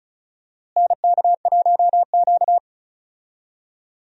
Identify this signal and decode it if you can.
Morse code, heard as a keyed tone, about 36 wpm, 700 Hz. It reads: NK1Q
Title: USA Call Signs - 4 Characters - Repeated 3x - 35wpm